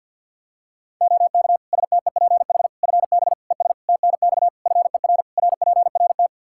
Morse code, heard as a keyed tone, about 53 wpm, 700 Hz. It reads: OK STEPH FB ES TNX FER RPRT
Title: Practice QSO - Exchange 2 - Info - 50wpm